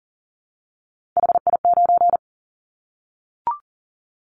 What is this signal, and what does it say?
Morse code, heard as a keyed tone, about 38 wpm, 700 Hz. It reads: HI9
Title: Callsign Prefixes for Top 100 DXCC - 40wpm